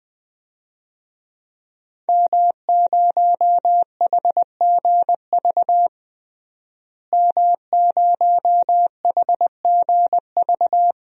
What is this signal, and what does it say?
Morse code, heard as a keyed tone, about 21 wpm, 700 Hz. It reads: M0HGV M0HGV
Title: DXCC-100 Call Signs - 5 Characters - Repeated 3x - 20wpm